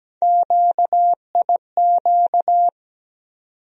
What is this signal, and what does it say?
Morse code, heard as a keyed tone, about 17 wpm, 700 Hz. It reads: QIQ
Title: USA Call Sign Postfixes - 3 Letters 17wpm